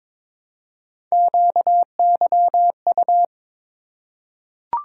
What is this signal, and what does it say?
Morse code, heard as a keyed tone, about 22 wpm, 700 Hz. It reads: QYU
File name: practice / 22wpm / Three Letter-Number Combinations - 22wpm